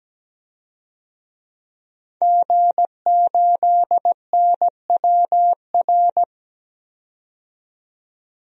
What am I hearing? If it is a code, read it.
Morse code, heard as a keyed tone, about 17 wpm, 700 Hz. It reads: G8NWR
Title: DXCC-100 Call Signs - 5 Characters - 17wpm